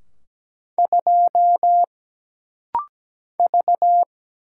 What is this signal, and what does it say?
Morse code, heard as a keyed tone, about 17 wpm, 700 Hz. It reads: 2 V
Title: Single Letter-Number Alt - Rapid-Fire 17wpm